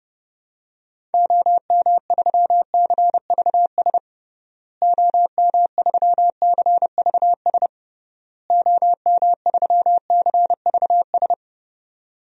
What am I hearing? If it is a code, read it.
Morse code, heard as a keyed tone, about 30 wpm, 700 Hz. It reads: OM3CVS OM3CVS OM3CVS
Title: DXCC-100 Call Signs - 6 Characters - Repeated 3x - 30wpm